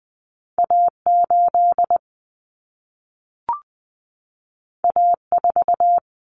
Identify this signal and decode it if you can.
Morse code, heard as a keyed tone, about 20 wpm, 700 Hz. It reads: A8 A4
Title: Common Cut Numbers - 2-Digit 20wpm